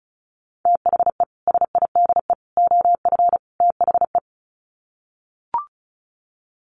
Morse code, heard as a keyed tone, about 34 wpm, 700 Hz. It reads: THE SIDE OF THE
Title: Sets of 4 Words 35wpm